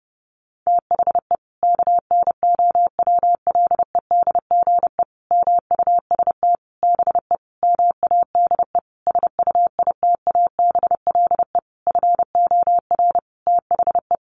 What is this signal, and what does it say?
Morse code, heard as a keyed tone, about 30 wpm, 700 Hz. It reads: THE KNOWLEDGE MUST BE MADE SUITABLE FOR THE
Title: Sentences from Top 1400 Words 30wpm